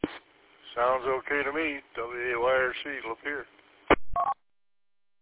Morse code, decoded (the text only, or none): none